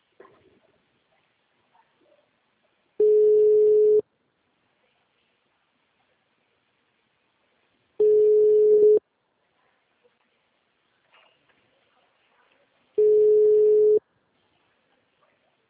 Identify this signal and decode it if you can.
Ringback tone (Europe, 425 Hz).